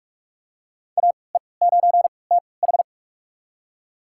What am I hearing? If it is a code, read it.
Morse code, heard as a keyed tone, about 46 wpm, spaced as for 25 wpm, 700 Hz. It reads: AE9TH
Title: USA Call Signs - 5 Characters - 3x Character Spacing - 45wpm